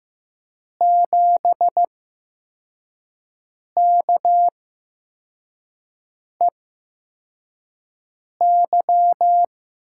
Morse code, heard as a keyed tone, about 15 wpm, spaced as for 4 wpm, 700 Hz. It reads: 7KEY
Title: USA Call Signs - 5 Characters - 8x Character Spacing - 15wpm